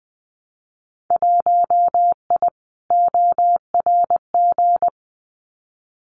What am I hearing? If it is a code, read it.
Morse code, heard as a keyed tone, about 20 wpm, 700 Hz. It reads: 1I ORG
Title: Winter Field Day Exchanges Encore 20wpm